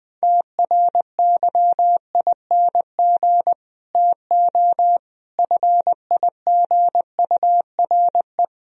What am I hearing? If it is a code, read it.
Morse code, heard as a keyed tone, about 20 wpm, 700 Hz. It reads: TRYING TO FIGURE
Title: Speed-Racing - Sets of 5 Words 20wpm